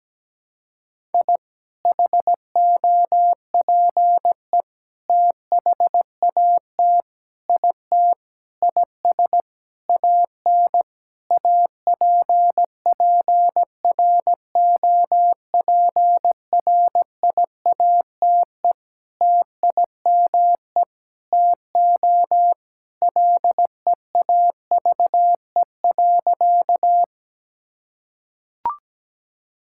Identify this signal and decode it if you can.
Morse code, heard as a keyed tone, about 17 wpm, 700 Hz. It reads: I HOPE THAT IT IS AN APPROPRIATE TIME TO LEAVE.
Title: Sentences from Top 1000 Words - Review Part 5 - 17wpm